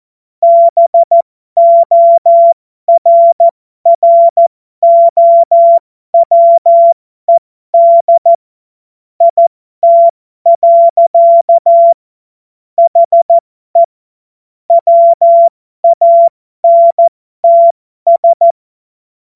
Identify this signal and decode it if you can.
Morse code, heard as a keyed tone, about 14 wpm, spaced as for 12 wpm, 680 Hz. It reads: BORROWED IT. HE WANTS
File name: Morsekurs / Witze